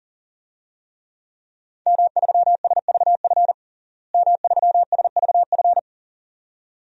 Morse code, heard as a keyed tone, about 40 wpm, 700 Hz. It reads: M3SVF M3SVF
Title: DXCC-100 Call Signs - 5 Characters - Repeated 2x - 40wpm